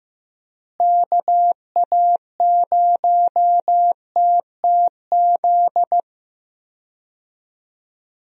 Morse code, heard as a keyed tone, about 15 wpm, 700 Hz. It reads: KA0TTZ
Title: USA Call Signs - 6 Characters - Repeated 2x - 15wpm